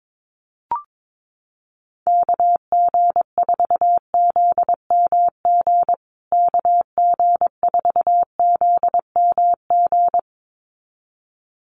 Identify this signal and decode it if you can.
Morse code, heard as a keyed tone, about 22 wpm, 700 Hz. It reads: KG4ZMG KG4ZMG
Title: USA Call Signs Repeated 2x 22wpm